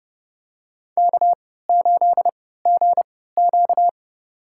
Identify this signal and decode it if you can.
Morse code, heard as a keyed tone, about 28 wpm, spaced as for 17 wpm, 700 Hz. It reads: K8GQ
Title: USA Call Signs - 4 Characters - 3x Character Spacing - 30wpm